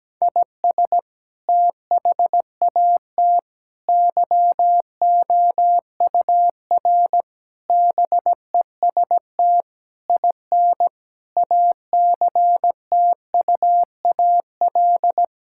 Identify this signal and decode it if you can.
Morse code, heard as a keyed tone, about 17 wpm, 700 Hz. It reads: IS THAT YOUR BEST IN ACTUAL
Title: Sentences from Top 1000 Words - Review Part 5 - 17wpm